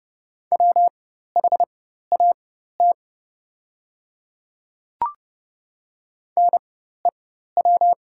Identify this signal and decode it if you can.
Morse code, heard as a keyed tone, about 30 wpm, spaced as for 14 wpm, 700 Hz. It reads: WHAT NEW
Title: 4x Character Spacing - Top 200 Words - 30wpm